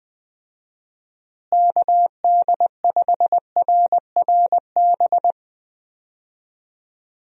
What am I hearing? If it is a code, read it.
Morse code, heard as a keyed tone, about 20 wpm, 700 Hz. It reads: KD5RRB
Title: USA Call Signs - 6 Characters - Repeated 2x - 20wpm